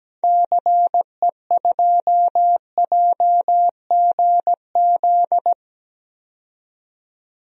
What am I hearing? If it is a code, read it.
Morse code, heard as a keyed tone, about 17 wpm, 700 Hz. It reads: CE2JGZ
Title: DXCC-100 Call Signs - 6 Characters - 17wpm